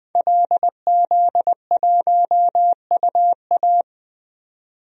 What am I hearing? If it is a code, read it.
Morse code, heard as a keyed tone, about 21 wpm, 700 Hz. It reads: LZ1UA
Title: DXCC-100 Call Signs - 5 Characters - 20wpm